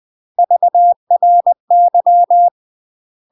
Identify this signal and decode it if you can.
Morse code, heard as a keyed tone, about 20 wpm, 700 Hz. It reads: VRY
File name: UK MASTER QSOs TRAINING / 20WPM / UK MASTER QSOs 0053